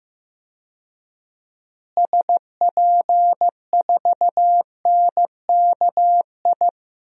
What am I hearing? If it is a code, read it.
Morse code, heard as a keyed tone, about 15 wpm, 700 Hz. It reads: SP4NKI
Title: DXCC-100 Call Signs - 6 Characters - Repeated 2x - 15wpm